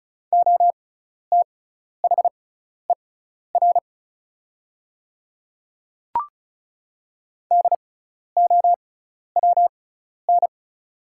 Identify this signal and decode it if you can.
Morse code, heard as a keyed tone, about 36 wpm, spaced as for 12 wpm, 700 Hz. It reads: OTHER DOWN
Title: 6x Character Spacing - Top 100 Words - 35wpm